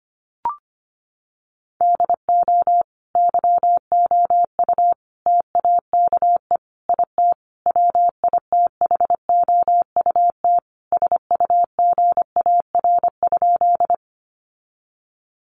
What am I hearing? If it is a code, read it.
Morse code, heard as a keyed tone, about 25 wpm, 700 Hz. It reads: DO YOU TAKE IT WITHOUT SUGAR?
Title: Sentences from Top 1200 Words 25wpm